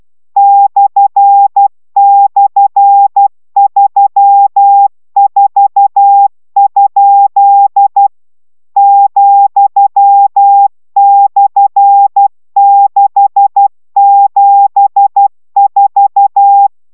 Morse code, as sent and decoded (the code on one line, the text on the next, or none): -..-. -..-. ...-- ....- ..--.. / --..-- -..-. -.... --... ....-
//34? ,/674